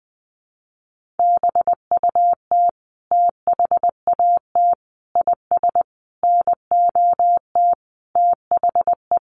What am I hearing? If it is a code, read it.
Morse code, heard as a keyed tone, about 20 wpm, 700 Hz. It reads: BUT THAT IS NOT THE
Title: Sentences from Top 600 Words 20wpm